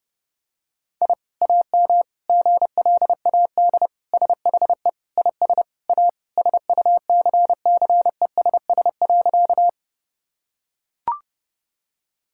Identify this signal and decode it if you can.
Morse code, heard as a keyed tone, about 30 wpm, 700 Hz. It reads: I AM GLAD SHE IS A SUCCESS.
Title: Sentences from Top 1000 Words - Review Part 6 - 30wpm